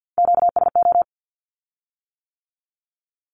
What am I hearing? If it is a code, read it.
Morse code, heard as a keyed tone, about 50 wpm, 700 Hz. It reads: QSO
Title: Q-Codes with Spoken Meaning - 50wpm